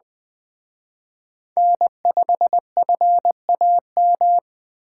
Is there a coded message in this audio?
Yes: Morse code from 1.6 to 4.4 s, heard as a keyed tone, about 21 wpm, 700 Hz. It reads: N5FAM